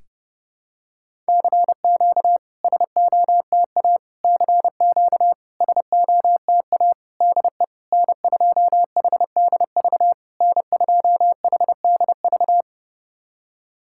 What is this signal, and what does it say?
Morse code, heard as a keyed tone, about 30 wpm, 690 Hz. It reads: CQ SOTA CQ SOTA DE N2HDV N2HDV
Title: SOTA QSO Practice 30wpm